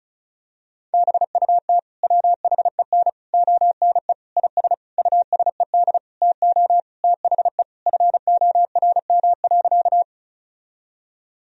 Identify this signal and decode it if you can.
Morse code, heard as a keyed tone, about 36 wpm, 700 Hz. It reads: BUT WHEN ONE IS USED TO THE FORM.
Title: Sentences from Top 1000 Words - Review Part 1 - 35wpm